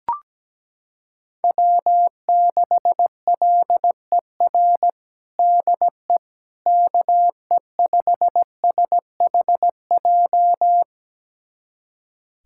Morse code, heard as a keyed tone, about 17 wpm, 700 Hz. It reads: W6LER DE KE5SHJ
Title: USA Callsign DE Callsign 17wpm